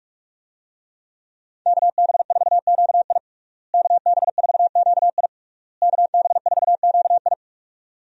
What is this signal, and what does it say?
Morse code, heard as a keyed tone, about 41 wpm, 700 Hz. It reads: KB4QI KB4QI KB4QI
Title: USA Call Signs - 5 Characters - Repeated 3x - 45wpm